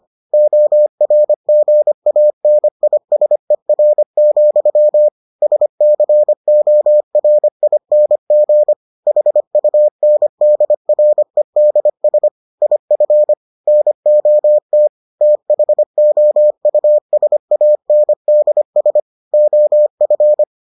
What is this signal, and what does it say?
Morse code, heard as a keyed tone, about 25 wpm, 600 Hz. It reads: ORGANISER, SCORING HUNDREDS IF NOT THOUSANDS OF